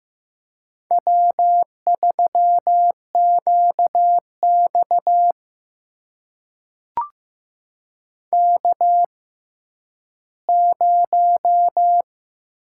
Morse code, heard as a keyed tone, about 15 wpm, spaced as for 10 wpm, 700 Hz. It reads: W3QX K0
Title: USA Call Signs - 6x Character Spacing - 15wpm